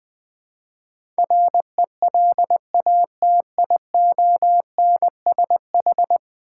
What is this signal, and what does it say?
Morse code, heard as a keyed tone, about 20 wpm, 700 Hz. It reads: RELATIONSH
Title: Build A Word - 5 Letters At A Time - 12-Letter Words - 20wpm